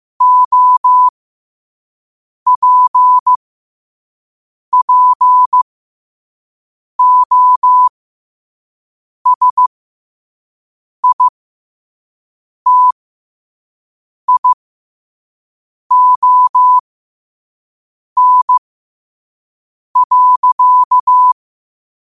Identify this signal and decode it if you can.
Morse code, heard as a keyed tone, about 15 wpm, spaced as for 5 wpm, 1000 Hz. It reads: OPPOSITION.